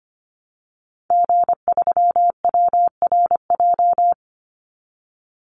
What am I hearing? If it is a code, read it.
Morse code, heard as a keyed tone, about 25 wpm, 700 Hz. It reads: G3WRJ
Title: Call Signs - 25wpm